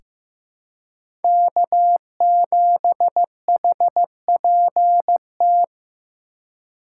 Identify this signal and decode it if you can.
Morse code, heard as a keyed tone, about 16 wpm, 700 Hz. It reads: K7HPT
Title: K1USN SST QSO Practice - 15wpm